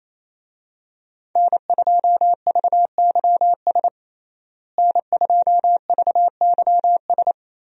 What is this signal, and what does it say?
Morse code, heard as a keyed tone, about 28 wpm, 700 Hz. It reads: N2VYS N2VYS
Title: USA Call Signs - 5 Characters - Repeated 2x - 28wpm